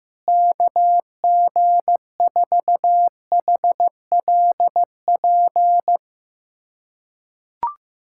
Morse code, heard as a keyed tone, about 16 wpm, 700 Hz. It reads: KG4HLP
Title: USA Callsign DE Callsign - 6 Characters - 15wpm